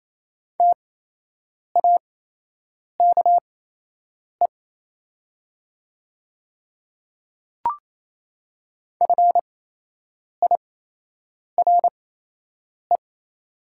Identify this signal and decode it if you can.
Morse code, heard as a keyed tone, about 27 wpm, spaced as for 8 wpm, 700 Hz. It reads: TAKE FIRE
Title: 8x Character Spacing - Top 500 Words - 28wpm